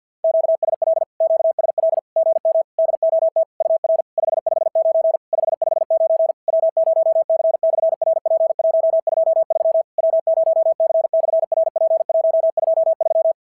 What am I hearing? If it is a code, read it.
Morse code, heard as a keyed tone, about 50 wpm, 650 Hz. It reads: QSL QSL GM DOT UR 559 559 W0C/RP123 W0C/RP123
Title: SOTA QSO Practice 50wpm